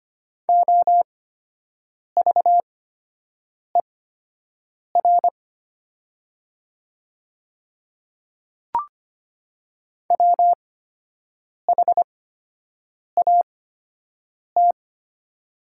Morse code, heard as a keyed tone, about 25 wpm, spaced as for 7 wpm, 700 Hz. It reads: OVER WHAT
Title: 8x Character Spacing - Top 200 Words - 25wpm